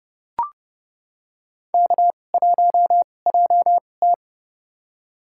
Morse code, heard as a keyed tone, about 29 wpm, spaced as for 21 wpm, 700 Hz. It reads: K1JT
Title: USA Call Signs - 2x Character Spacing - 30wpm